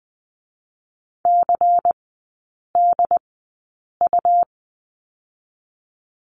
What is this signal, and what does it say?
Morse code, heard as a keyed tone, about 20 wpm, 700 Hz. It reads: C D U
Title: 121 Lesson 20e - Number 3 - Three Characters 20wpm